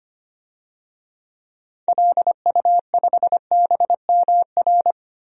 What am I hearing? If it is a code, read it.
Morse code, heard as a keyed tone, about 25 wpm, 700 Hz. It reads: LU5BMR
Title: DXCC-100 Call Signs - 6 Characters - 25wpm